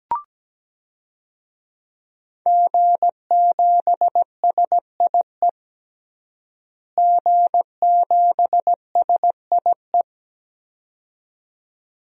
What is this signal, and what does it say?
Morse code, heard as a keyed tone, about 17 wpm, 700 Hz. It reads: G7SIE G7SIE
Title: DXCC-100 Call Signs - 5 Characters - Repeated 2x - 17wpm